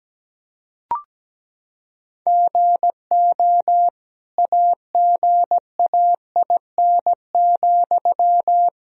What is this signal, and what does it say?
Morse code, heard as a keyed tone, about 17 wpm, 700 Hz. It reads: GO AGAIN,
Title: Sentences from Top 1000 Words - Review Part 4 - 17wpm